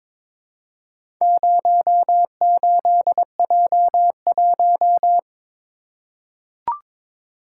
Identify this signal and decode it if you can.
Morse code, heard as a keyed tone, about 22 wpm, 700 Hz. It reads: 08J1
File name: practice / 22wpm / Four Letter-Number Combinations - 22wpm